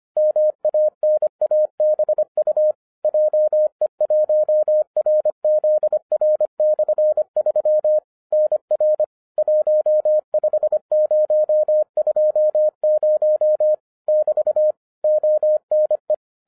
Morse code, heard as a keyed tone, about 25 wpm, 610 Hz. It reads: MANABU JE1RZR/3 NR 15020 <BT> ONE